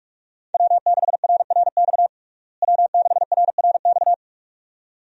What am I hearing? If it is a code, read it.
Morse code, heard as a keyed tone, about 46 wpm, 700 Hz. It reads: W6RRX W6RRX
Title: USA Call Signs - 5 Characters - Repeated 2x - 45wpm